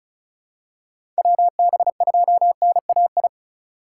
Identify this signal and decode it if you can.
Morse code, heard as a keyed tone, about 36 wpm, 700 Hz. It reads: WB2NAI
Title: USA Call Signs - 2x Character Spacing - 35wpm